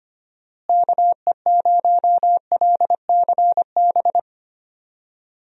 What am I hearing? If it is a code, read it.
Morse code, heard as a keyed tone, about 25 wpm, 700 Hz. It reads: KE0LCB
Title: USA Call Signs - 6 Characters - Repeated 3x - 25wpm